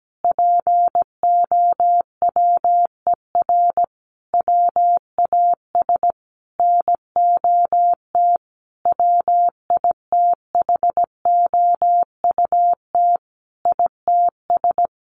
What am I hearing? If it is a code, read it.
Morse code, heard as a keyed tone, about 17 wpm, 700 Hz. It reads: POWER WAS NOT WITHOUT ITS
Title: Sentences from Top 1500 Words 17wpm